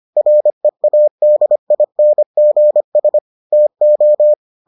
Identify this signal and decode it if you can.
Morse code, heard as a keyed tone, about 25 wpm, 600 Hz. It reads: READINGS TO